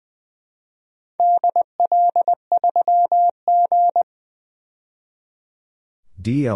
Morse code, heard as a keyed tone, about 20 wpm, 700 Hz. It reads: DL3G
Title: DXCC-100 Call Signs - 4 Characters - 20wpm